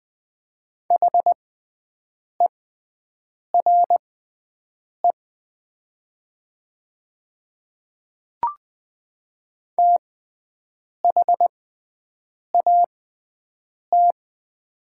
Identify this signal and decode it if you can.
Morse code, heard as a keyed tone, about 21 wpm, spaced as for 7 wpm, 700 Hz. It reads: HERE THAT
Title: 6x Character Spacing - Top 500 Words - 20wpm